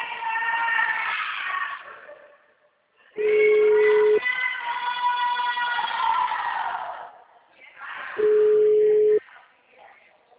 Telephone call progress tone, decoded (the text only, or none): ringback tone